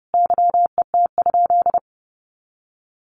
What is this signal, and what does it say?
Morse code, heard as a keyed tone, about 30 wpm, 700 Hz. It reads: YET?
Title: Sentences from Top 1400 Words 30wpm